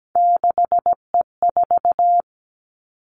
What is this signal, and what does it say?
Morse code, heard as a keyed tone, about 17 wpm, 700 Hz. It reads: 6E4
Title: Three Challenging Letter-Number Combinations 17wpm